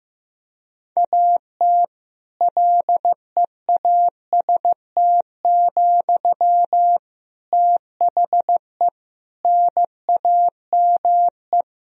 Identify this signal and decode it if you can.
Morse code, heard as a keyed tone, about 15 wpm, 700 Hz. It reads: AT LEAST, THE NAME
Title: Sentences from Top 1000 Words - Review Part 4 - 15wpm